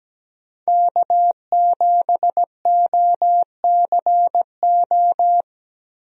Morse code, heard as a keyed tone, about 17 wpm, 700 Hz. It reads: K7OCO
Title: USA Call Signs - 5 Characters - Repeated 3x - 17wpm